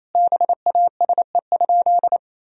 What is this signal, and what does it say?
Morse code, heard as a keyed tone, about 28 wpm, 700 Hz. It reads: BASE?
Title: Sentences from Top 1000 Words - Review Part 1 - 28wpm